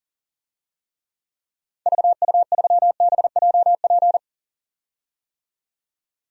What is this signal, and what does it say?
Morse code, heard as a keyed tone, about 40 wpm, 700 Hz. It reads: VU3BJP
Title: DXCC-100 Call Signs - 6 Characters - 40wpm